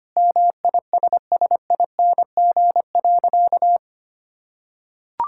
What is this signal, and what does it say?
Morse code, heard as a keyed tone, about 25 wpm, 700 Hz. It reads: MISSING.
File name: practice / 25wpm / Sentences with 4-Word Phrases - 25wpm